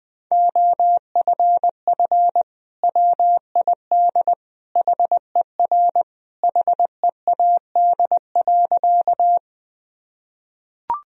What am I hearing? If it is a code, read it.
Morse code, heard as a keyed tone, about 20 wpm, 700 Hz. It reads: OFF WID HER HEAD.
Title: Sentences from Top 300 Words and Abbreviations 20wpm